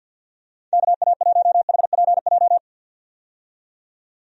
Morse code, heard as a keyed tone, about 51 wpm, 700 Hz. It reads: KA1HPJ
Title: USA Call Signs - 6 Characters - 50wpm